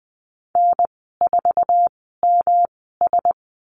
Morse code, heard as a keyed tone, about 20 wpm, spaced as for 14 wpm, 700 Hz. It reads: N4MS
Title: 122 Lesson 20g - Number 3 - Call Signs 20wpm